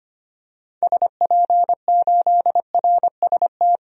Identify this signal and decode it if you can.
Morse code, heard as a keyed tone, about 25 wpm, 700 Hz. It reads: SP8RST
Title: DXCC-100 Call Signs - 6 Characters - 25wpm